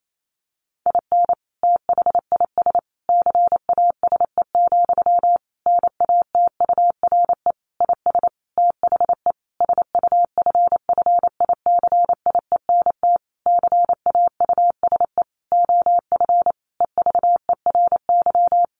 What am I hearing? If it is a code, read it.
Morse code, heard as a keyed tone, about 28 wpm, 700 Hz. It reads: IN THIS CASE, NATURE IS THE SUFFICIENT CAUSE OF EVERY
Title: Sentences from Top 1400 Words 28wpm